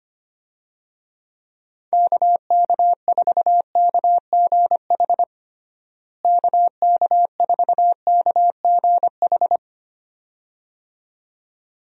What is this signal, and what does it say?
Morse code, heard as a keyed tone, about 25 wpm, 700 Hz. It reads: KK4KGH KK4KGH
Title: USA Call Signs - 6 Characters - Repeated 2x - 25wpm